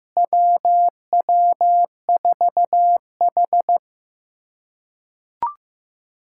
Morse code, heard as a keyed tone, about 16 wpm, 700 Hz. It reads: WW4H
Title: USA Callsign DE Callsign - 4 Characters - 15wpm